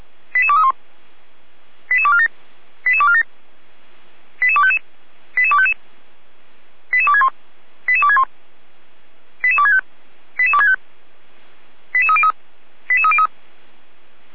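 Five-tone selective call (ZVEI), heard as quick runs of five tones, tones of about 70 ms. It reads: 88231 88247 88247 88255 88255 88261 88261 88265 88265 88303 88303
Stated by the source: ZVEI five-tone selective call received on SDR.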